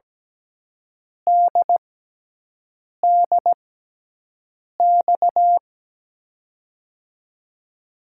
Morse code, heard as a keyed tone, about 17 wpm, spaced as for 6 wpm, 700 Hz. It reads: DDX
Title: USA Call Signs - 6 Characters - 6x Character Spacing - 17wpm